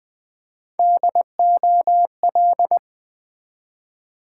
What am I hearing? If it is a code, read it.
Morse code, heard as a keyed tone, about 21 wpm, 700 Hz. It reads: DOL
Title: Build A Word - 3 Letters At A Time - 6-Letter Words - 20wpm